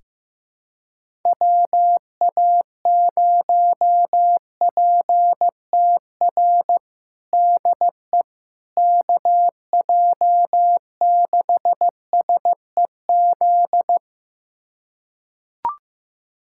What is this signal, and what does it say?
Morse code, heard as a keyed tone, about 15 wpm, 700 Hz. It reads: WA0PTR DE KJ6SEZ
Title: USA Callsign DE Callsign - 6 Characters - 15wpm